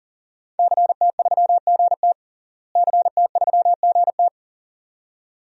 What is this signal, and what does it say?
Morse code, heard as a keyed tone, about 40 wpm, 700 Hz. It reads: CT3GT CT3GT
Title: DXCC-100 Call Signs - 5 Characters - Repeated 2x - 40wpm